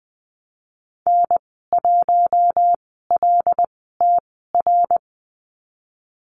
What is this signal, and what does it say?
Morse code, heard as a keyed tone, about 20 wpm, spaced as for 15 wpm, 700 Hz. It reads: N1LTR
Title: 076 Lesson 13g - Letter L - Call Signs 20wpm